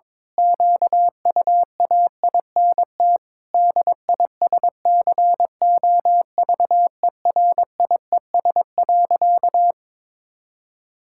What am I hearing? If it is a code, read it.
Morse code, heard as a keyed tone, about 21 wpm, 700 Hz. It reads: QUAINT DISCOVERIES.
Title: Pangram Challenge v2 - 22wpm